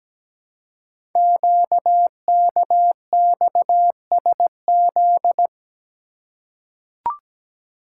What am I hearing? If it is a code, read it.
Morse code, heard as a keyed tone, about 17 wpm, 700 Hz. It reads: QKXSZ
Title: Code Groups - 5 Letters - 17wpm